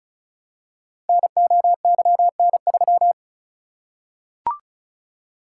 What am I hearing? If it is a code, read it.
Morse code, heard as a keyed tone, about 36 wpm, 690 Hz. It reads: NOYN3
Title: Code Groups - 5 Letter:Numbers - 35wpm